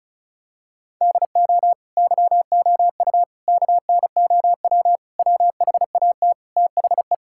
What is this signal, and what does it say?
Morse code, heard as a keyed tone, about 36 wpm, 700 Hz. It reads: DO YOU KNOW WHAT THE
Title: Sentences from Top 1000 Words - Review Part 6 - 35wpm